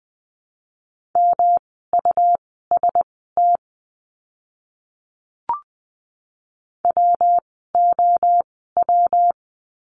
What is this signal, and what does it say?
Morse code, heard as a keyed tone, about 20 wpm, spaced as for 14 wpm, 700 Hz. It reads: MUST WOW
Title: 115 Lesson 19f - Letter w - Words 20wpm